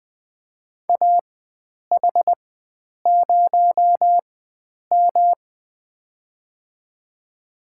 Morse code, heard as a keyed tone, about 20 wpm, spaced as for 9 wpm, 700 Hz. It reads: AH0M
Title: USA Call Signs - 4 Characters - 4x Character Spacing - 20wpm